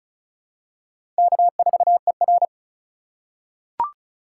Morse code, heard as a keyed tone, about 36 wpm, 700 Hz. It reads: K4ER
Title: USA Call Signs - 4 Characters - 4x Character Spacing - 35wpm